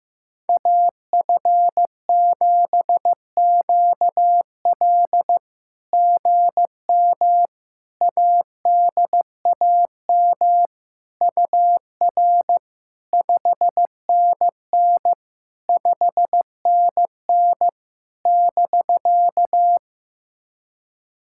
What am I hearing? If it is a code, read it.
Morse code, heard as a keyed tone, about 17 wpm, 700 Hz. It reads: AF7QL GM ADAM UR 5NN 5NN <BK>
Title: SOTA QSO Practice 15wpm